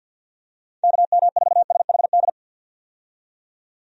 Morse code, heard as a keyed tone, about 51 wpm, 700 Hz. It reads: KM4SHD
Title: USA Call Signs - 6 Characters - 50wpm